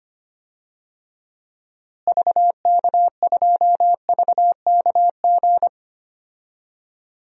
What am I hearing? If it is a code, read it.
Morse code, heard as a keyed tone, about 25 wpm, 700 Hz. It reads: VK2VKG
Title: DXCC-100 Call Signs - 6 Characters - Repeated 2x - 25wpm